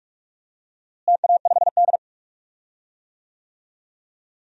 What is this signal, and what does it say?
Morse code, heard as a keyed tone, about 46 wpm, 700 Hz. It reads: TA5D
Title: DXCC-100 Call Signs - 4 Characters - Repeated 2x - 45wpm